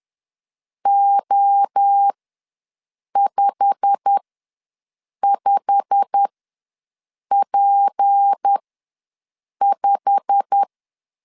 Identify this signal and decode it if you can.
Morse code, heard as a keyed tone, about 10 wpm, spaced as for 6 wpm, 800 Hz. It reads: O55P5
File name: LEZIONI / 8 YJ45 / 8 YJ45 ws 10wpm 3843